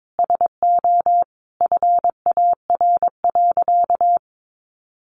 Morse code, heard as a keyed tone, about 22 wpm, 700 Hz. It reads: SO FAR.
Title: Sentences from Top 800 Words 22wpm